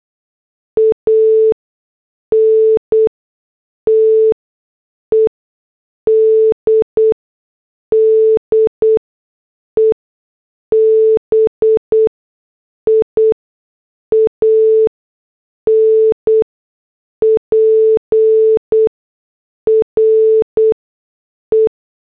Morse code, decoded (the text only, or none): ANTED DEBIAN PRE